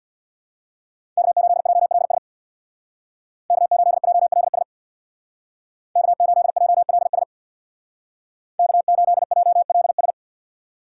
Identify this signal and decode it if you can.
Morse code, heard as a keyed tone, about 61 wpm, 700 Hz. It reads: K8JLS K8JLS K8JLS K8JLS
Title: Speed Racing - USA Call Signs 50wpm